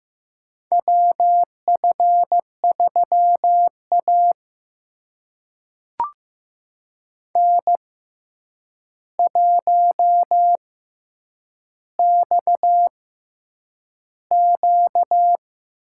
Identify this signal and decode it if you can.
Morse code, heard as a keyed tone, about 15 wpm, spaced as for 8 wpm, 700 Hz. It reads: WF3A N1XQ
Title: USA Call Signs - 4 Characters - 6x Character Spacing - 15wpm